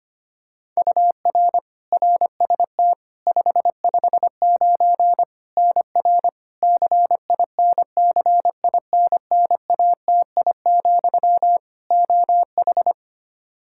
Morse code, heard as a keyed tone, about 25 wpm, 700 Hz. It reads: UR RST 559 NR CINCINNATI, OH
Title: K1USN SST QSO Practice 25wpm